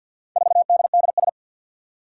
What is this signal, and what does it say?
Morse code, heard as a keyed tone, about 48 wpm, 700 Hz. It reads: 4DDS
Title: Four Challenging Letter-Number Combinations 50wpm